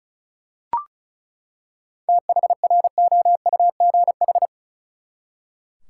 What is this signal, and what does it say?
Morse code, heard as a keyed tone, about 36 wpm, 700 Hz. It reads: THROUGH
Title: Seven Letter Words - 35wpm